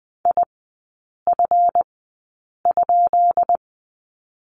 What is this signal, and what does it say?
Morse code, heard as a keyed tone, about 20 wpm, spaced as for 8 wpm, 700 Hz. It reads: IF?
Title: 138 Lesson 23e - Letter f - Three Characters 20wpm